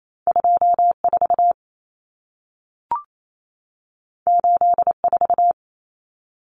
Morse code, heard as a keyed tone, about 28 wpm, 700 Hz. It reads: 24 84
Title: Numbers - 2-Digit 28wpm